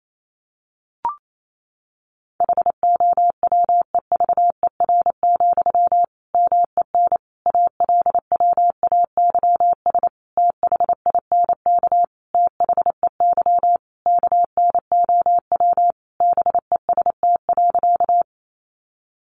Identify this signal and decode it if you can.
Morse code, heard as a keyed tone, about 28 wpm, 700 Hz. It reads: HOWEVER, MEN ALWAYS THINK THEY KNOW BEST.
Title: Sentences from Top 600 Words 28wpm